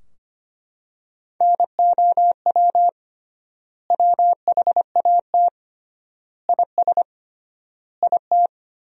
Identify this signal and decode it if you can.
Morse code, heard as a keyed tone, about 25 wpm, 700 Hz. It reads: NOW WHAT IS IT
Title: 3x Word-Spacing - Sentences from Top 200 Words - 25wpm